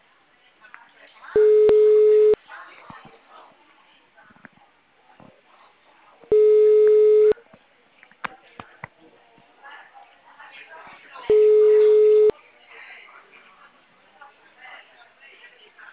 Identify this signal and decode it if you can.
Ringback tone (Europe, 425 Hz), with speech.
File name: out- -2017- - - .88